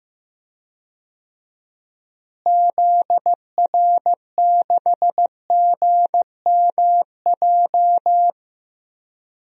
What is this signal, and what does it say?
Morse code, heard as a keyed tone, about 15 wpm, 700 Hz. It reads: ZR6GMJ
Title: DXCC-100 Call Signs - 6 Characters - Repeated 3x - 15wpm